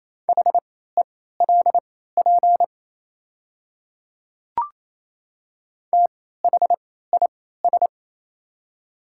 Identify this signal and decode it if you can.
Morse code, heard as a keyed tone, about 28 wpm, spaced as for 16 wpm, 700 Hz. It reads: HELP THIS
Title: 3x Character Spacing - Top 300 Words - 28wpm